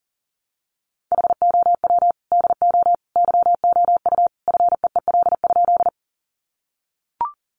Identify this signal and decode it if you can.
Morse code, heard as a keyed tone, about 40 wpm, 700 Hz. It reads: HOW DO YOU FEEL?
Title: Sentences from Top 400 Words 40wpm